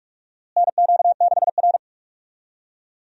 Morse code, heard as a keyed tone, about 46 wpm, 700 Hz. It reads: NQ6R
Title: USA Call Signs - 4 Characters - Repeated 2x - 45wpm